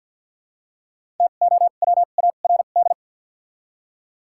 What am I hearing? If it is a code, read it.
Morse code, heard as a keyed tone, about 53 wpm, spaced as for 37 wpm, 700 Hz. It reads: TOWARD
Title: 2x Character Spacing - Top 500 Words - 50wpm